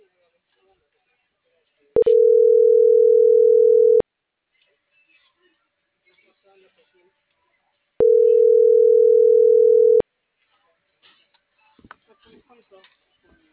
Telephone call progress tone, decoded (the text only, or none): ringback tone